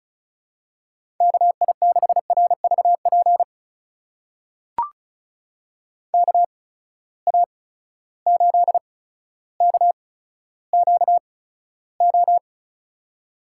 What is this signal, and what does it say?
Morse code, heard as a keyed tone, about 36 wpm, spaced as for 15 wpm, 700 Hz. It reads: KI6RVP KA8KQO
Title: USA Call Signs - 6 Characters - 8x Character Spacing - 35wpm